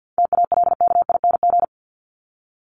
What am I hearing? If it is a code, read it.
Morse code, heard as a keyed tone, about 50 wpm, 700 Hz. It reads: TALKING
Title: ING Suffix - Top 100 Words with Base Words 50wpm